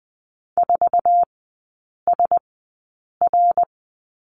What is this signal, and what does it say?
Morse code, heard as a keyed tone, about 20 wpm, spaced as for 8 wpm, 700 Hz. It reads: 4SR
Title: 053 Lesson 10e - Letter r - Three Characters 20wpm